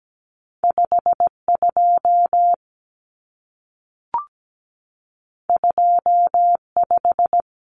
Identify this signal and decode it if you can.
Morse code, heard as a keyed tone, about 17 wpm, 700 Hz. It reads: 52 25
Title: Numbers - 2-Digit 17wpm